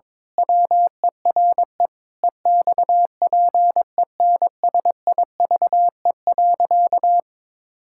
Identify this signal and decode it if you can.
Morse code, heard as a keyed tone, about 22 wpm, 700 Hz. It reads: WERE EXPENSIVE.
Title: Pangram Challenge v2 - 22wpm